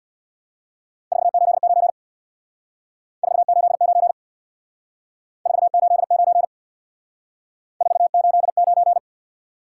Morse code, heard as a keyed tone, about 66 wpm, 700 Hz. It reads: 489 489 489 489